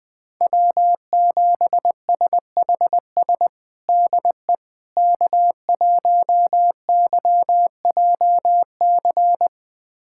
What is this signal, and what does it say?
Morse code, heard as a keyed tone, about 20 wpm, 700 Hz. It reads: W7SHS DE K1YJC
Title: USA Callsign DE Callsign - 5 Characters - 20wpm